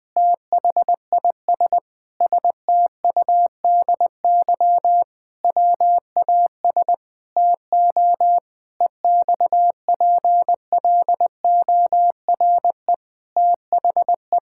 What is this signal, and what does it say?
Morse code, heard as a keyed tone, about 19 wpm, 700 Hz. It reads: THIS STUDY WAS TO EXPLORE THE
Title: Sentences with 5-Word Phrases - 20wpm